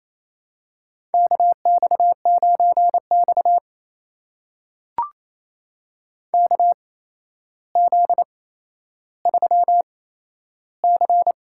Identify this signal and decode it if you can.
Morse code, heard as a keyed tone, about 28 wpm, spaced as for 12 wpm, 700 Hz. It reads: KX9X KZ3C
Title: USA Call Signs - 4 Characters - 8x Character Spacing - 28wpm